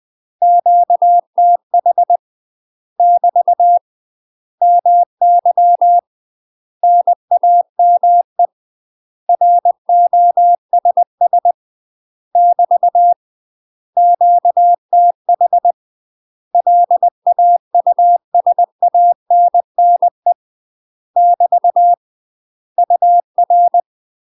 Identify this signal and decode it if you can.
Morse code, heard as a keyed tone, about 20 wpm, 700 Hz. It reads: QTH <BT> MY NAME ROSS <BT> QTH LAUSANNE <BT> UR
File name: UK MASTER QSOs TRAINING / 20WPM / UK MASTER QSOs 0022